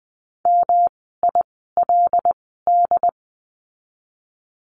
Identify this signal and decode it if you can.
Morse code, heard as a keyed tone, about 19 wpm, spaced as for 14 wpm, 700 Hz. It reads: MILD
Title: 108 Lesson 18f - Letter m - Words 20wpm